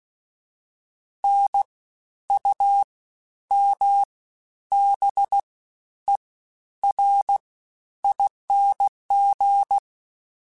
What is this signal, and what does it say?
Morse code, heard as a keyed tone, about 16 wpm, 800 Hz. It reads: N U M B E R ING
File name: Special INGs-mp3s / Special INGs-mp3s / Fast ing @15